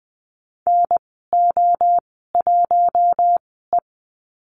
Morse code, heard as a keyed tone, about 20 wpm, spaced as for 15 wpm, 700 Hz. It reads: NO1E
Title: 055 Lesson 10g - Letter r - Call Signs 20wpm